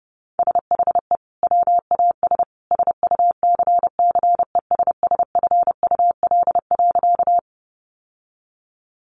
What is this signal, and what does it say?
Morse code, heard as a keyed tone, about 30 wpm, 700 Hz. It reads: SHE WAS SUCCESSFUL.
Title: Sentences from Top 900 Words 30wpm